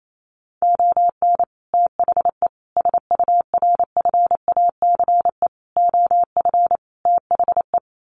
Morse code, heard as a keyed tone, about 28 wpm, 700 Hz. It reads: ON THE SURFACE OF THE
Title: Sets of 5 Words Encore 28wpm